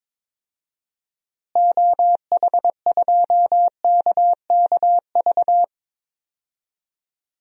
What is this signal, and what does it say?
Morse code, heard as a keyed tone, about 22 wpm, 700 Hz. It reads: OH2KKV
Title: DXCC-100 Call Signs - 6 Characters - Repeated 3x - 22wpm